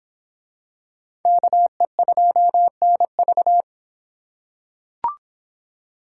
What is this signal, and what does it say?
Morse code, heard as a keyed tone, about 25 wpm, 700 Hz. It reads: KE2NV